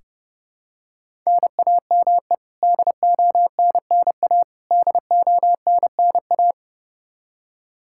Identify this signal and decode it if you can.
Morse code, heard as a keyed tone, about 30 wpm, 700 Hz. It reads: NAME DONNA DONNA
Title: K1USN SST QSO Practice 30wpm